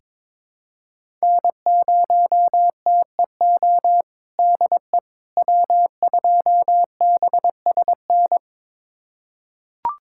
Morse code, heard as a keyed tone, about 22 wpm, 700 Hz. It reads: N0TEO DE W2BSN
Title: USA Callsign DE Callsign - 5 Characters - 22wpm